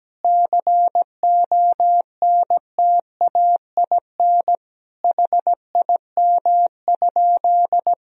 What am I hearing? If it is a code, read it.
Morse code, heard as a keyed tone, about 17 wpm, 700 Hz. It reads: CONTAIN HIM?
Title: Sentences from Top 1000 Words - Review Part 2 - 17wpm